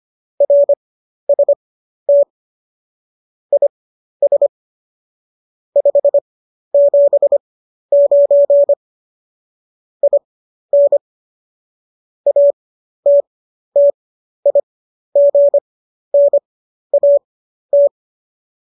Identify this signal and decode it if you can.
Morse code, heard as a keyed tone, about 25 wpm, spaced as for 12 wpm, 570 Hz. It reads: RST IS 579 IN ATTIGNAT
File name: qso009 12